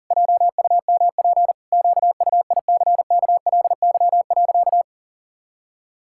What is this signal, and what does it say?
Morse code, heard as a keyed tone, about 40 wpm, 700 Hz. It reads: JUMP QUICKLY.
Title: Pangram Challenge v2 - 40wpm